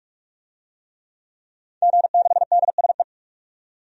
Morse code, heard as a keyed tone, about 44 wpm, 700 Hz. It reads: G6DSE
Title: DXCC-100 Call Signs - 5 Characters - 45wpm